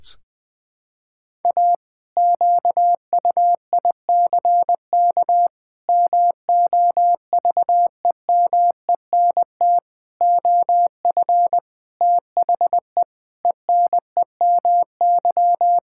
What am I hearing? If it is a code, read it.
Morse code, heard as a keyed tone, about 20 wpm, 700 Hz. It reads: A QUICK MOVEMENT OF THE ENEMY
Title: Pangram Challenge v2 - 20wpm